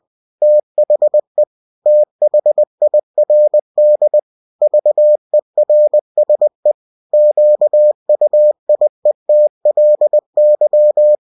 Morse code, heard as a keyed tone, about 20 wpm, 600 Hz. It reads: THE THIRD VERSE QUIETLY
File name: cw-books / winnie-the-pooh / winnie-20wpm-page-0046